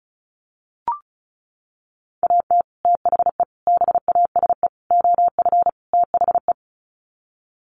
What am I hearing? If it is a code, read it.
Morse code, heard as a keyed tone, about 34 wpm, 700 Hz. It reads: AT THE BASE OF THE
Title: Sets of 5 Words 35wpm